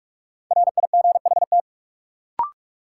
Morse code, heard as a keyed tone, about 43 wpm, 700 Hz. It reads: RIGHT
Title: Sets of 2 Words - New Word Drill - 45wpm